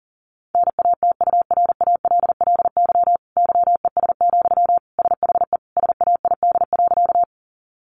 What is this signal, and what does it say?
Morse code, heard as a keyed tone, about 40 wpm, 700 Hz. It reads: NATURALLY YES, SHE SAID.
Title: Sentences from Top 1300 Words 40wpm